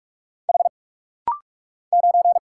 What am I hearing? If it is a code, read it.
Morse code, heard as a keyed tone, about 46 wpm, 700 Hz. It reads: H 9
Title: Single Letter-Number - Rapid-Fire 45wpm